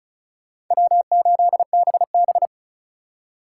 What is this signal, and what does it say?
Morse code, heard as a keyed tone, about 43 wpm, 700 Hz. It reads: W8BB
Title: Speed Racing - USA Call Signs 35wpm